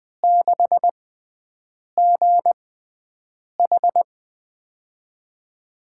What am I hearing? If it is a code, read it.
Morse code, heard as a keyed tone, about 21 wpm, spaced as for 7 wpm, 700 Hz. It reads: 6GH
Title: USA Call Signs - 6x Character Spacing - 20wpm